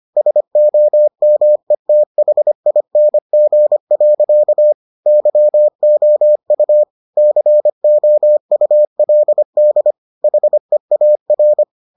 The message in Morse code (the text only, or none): SOMETHING. YOU COULD HEAR